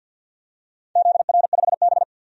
Morse code, heard as a keyed tone, about 51 wpm, 700 Hz. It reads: ZR5B
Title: DXCC-100 Call Signs - 4 Characters - 50wpm